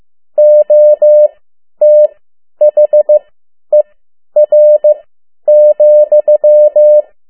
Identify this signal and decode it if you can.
Morse code, heard as a keyed tone, about 15 wpm, spaced as for 10 wpm, 600 Hz. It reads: OTHER,
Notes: Around the World in 80 Days paragraph one Morse code 10 WPM